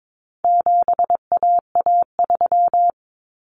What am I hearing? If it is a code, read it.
Morse code, heard as a keyed tone, about 22 wpm, 700 Hz. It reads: 7AA3
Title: Common Cut Numbers - 4-Digit 22wpm